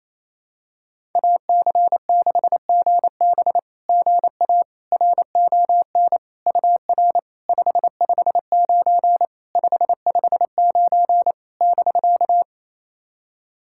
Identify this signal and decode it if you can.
Morse code, heard as a keyed tone, about 28 wpm, 700 Hz. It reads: AC6GB GA RON UR 559 559 <BK>
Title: SOTA QSO Practice 28wpm